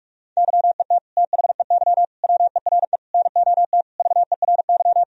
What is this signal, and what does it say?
Morse code, heard as a keyed tone, about 46 wpm, 700 Hz. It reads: YET THEY WERE NOT VERY
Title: Sentences from Top 1000 Words - Review Part 3 - 45wpm